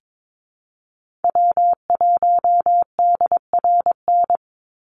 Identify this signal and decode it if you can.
Morse code, heard as a keyed tone, about 22 wpm, 700 Hz. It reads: W1DRN
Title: USA Call Signs 22wpm